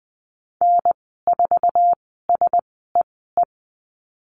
Morse code, heard as a keyed tone, about 20 wpm, spaced as for 14 wpm, 700 Hz. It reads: N4SEE
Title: 048 Lesson 9g - Number 4 - Call Signs 20wpm